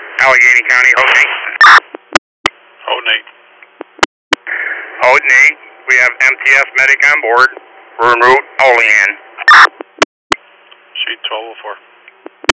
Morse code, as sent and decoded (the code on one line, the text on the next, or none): none